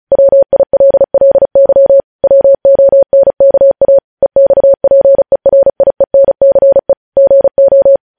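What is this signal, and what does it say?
Morse code, heard as a keyed tone, about 36 wpm, 550 Hz. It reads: WILLY WONKA EXPERIENCE GO
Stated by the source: Morse code news headlines sent at 35.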